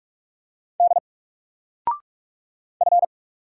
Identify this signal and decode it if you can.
Morse code, heard as a keyed tone, about 44 wpm, 700 Hz. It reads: D F
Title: Single Character (Letter:Number:Punctuation:Prosign) Alt 45wpm